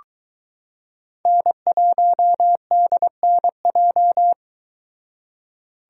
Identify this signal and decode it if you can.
Morse code, heard as a keyed tone, about 22 wpm, 700 Hz. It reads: N1DNJ